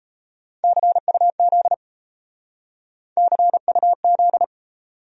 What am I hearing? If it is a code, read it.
Morse code, heard as a keyed tone, about 36 wpm, 700 Hz. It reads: CUZ CUZ